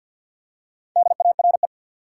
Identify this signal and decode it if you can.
Morse code, heard as a keyed tone, about 47 wpm, 700 Hz. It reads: DARE
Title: Four Letter Words - 50wpm